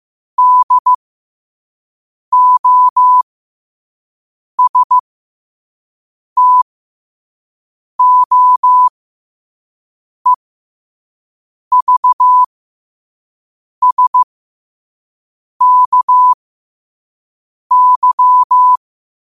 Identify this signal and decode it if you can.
Morse code, heard as a keyed tone, about 15 wpm, spaced as for 5 wpm, 1000 Hz. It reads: DOSTOEVSKY
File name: qotdfiles / 5wpm